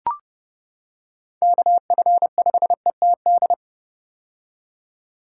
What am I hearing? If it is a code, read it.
Morse code, heard as a keyed tone, about 30 wpm, 700 Hz. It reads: KF5ETD
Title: K1USN SST QSO Practice - 30wpm